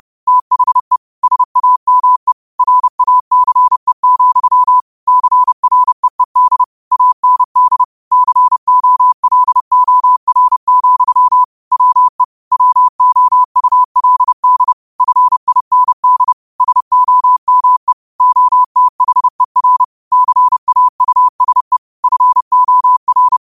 Morse code, heard as a keyed tone, about 30 wpm, 1000 Hz. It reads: THE SAME RACE, CREED AND COLOR, WE WOULD FIND SOME OTHER CAUSE FOR
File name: qotdfiles / 30wpm